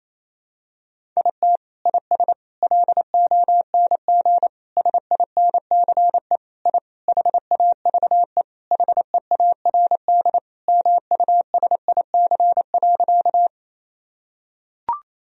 Morse code, heard as a keyed tone, about 27 wpm, 700 Hz. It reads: IT IS LONG SINCE I HAVE HEARD MUSIC.
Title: Sentences from Top 1000 Words - Review Part 1 - 28wpm